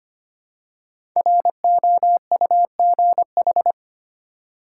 Morse code, heard as a keyed tone, about 25 wpm, 700 Hz. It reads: ROUGH
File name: 5 Letter Words - First 3 Letters Given - 25wpm